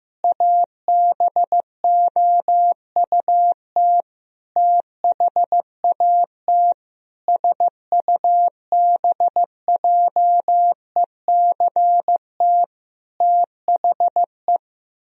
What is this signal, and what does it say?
Morse code, heard as a keyed tone, about 15 wpm, 700 Hz. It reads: ABOUT THAT SUBJECT THE
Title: Sentences from Top 1000 Words - Review Part 2 - 15wpm